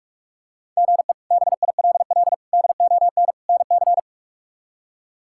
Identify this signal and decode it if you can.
Morse code, heard as a keyed tone, about 46 wpm, 700 Hz. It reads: GE BILL DON NC
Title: K1USN SST QSO Practice - 45wpm